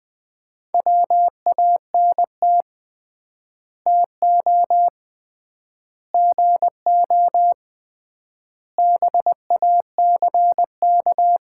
Morse code, heard as a keyed tone, about 21 wpm, 700 Hz. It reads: WANT TO GO BACK
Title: 3x Word-Spacing - Sets of 5 Words - 20wpm